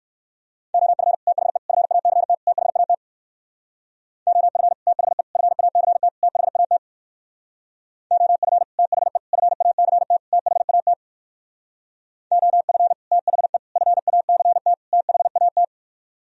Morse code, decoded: OF THE FACT THAT OF THE FACT THAT OF THE FACT THAT OF THE FACT THAT

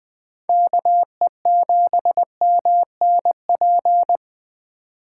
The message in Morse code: KE7MNP

